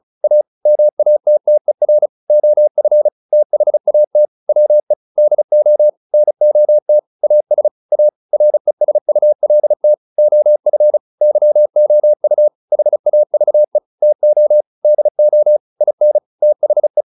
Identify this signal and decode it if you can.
Morse code, heard as a keyed tone, about 35 wpm, 600 Hz. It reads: A MATTER OF THAT WE DO NOT AS A RESULT OF YOU HAVE TO DO IN THE